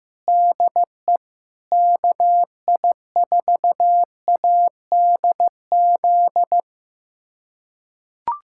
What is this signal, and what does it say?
Morse code, heard as a keyed tone, about 15 wpm, 700 Hz. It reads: DE KI4ADZ
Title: USA Callsign DE Callsign - 6 Characters - 15wpm